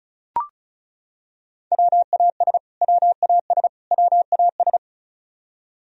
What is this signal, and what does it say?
Morse code, heard as a keyed tone, about 34 wpm, 700 Hz. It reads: WAS WAS WAS
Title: Top 100 Words Repeated 35wpm